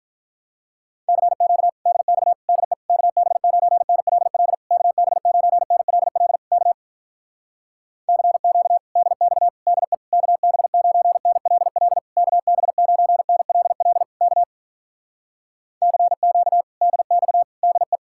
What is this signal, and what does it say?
Morse code, heard as a keyed tone, about 48 wpm, 700 Hz. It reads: CQ DX DE KB9NLL KB9NLL K CQ DX DE KB9NLL KB9NLL K CQ DX DE